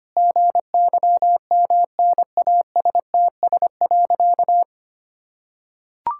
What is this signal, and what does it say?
Morse code, heard as a keyed tone, about 25 wpm, 700 Hz. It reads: GYMNASTS.
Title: Pangram Challenge v2 - 25wpm